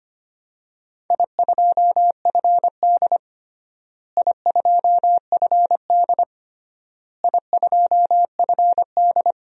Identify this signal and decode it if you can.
Morse code, heard as a keyed tone, about 25 wpm, 700 Hz. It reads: I2FD I2FD I2FD
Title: DXCC-100 Call Signs - 4 Characters - Repeated 3x - 25wpm